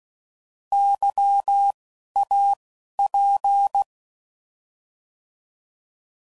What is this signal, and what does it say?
Morse code, heard as a keyed tone, about 16 wpm, spaced as for 11 wpm, 800 Hz. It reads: YAP